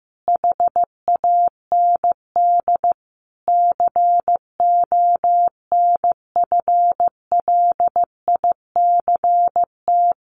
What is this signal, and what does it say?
Morse code, heard as a keyed tone, about 15 wpm, 700 Hz. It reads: HAND CONFLICT
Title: Sentences from Top 1300 Words 15wpm